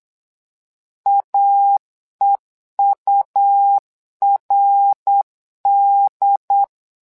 Morse code, heard as a keyed tone, about 8 wpm, 800 Hz. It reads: AEURD
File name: LEZIONI / 3 MWRO / 3 MWRO 8wpm 2606